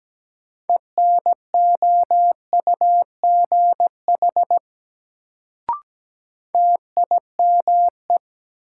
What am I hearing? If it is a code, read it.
Morse code, heard as a keyed tone, about 17 wpm, 700 Hz. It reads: ENOUGH TIME
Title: Sets of 2 Words - New Word Drill - 17wpm